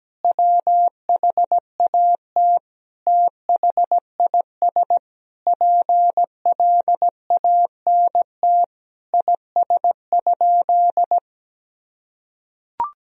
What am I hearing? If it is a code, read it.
Morse code, heard as a keyed tone, about 17 wpm, 700 Hz. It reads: WHAT THIS PLANT IS?
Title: Sentences from Top 1000 Words - Review Part 1 - 17wpm